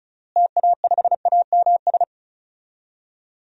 Morse code, heard as a keyed tone, about 34 wpm, 700 Hz. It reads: TA5AMS
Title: DXCC-100 Call Signs - 6 Characters - Repeated 3x - 35wpm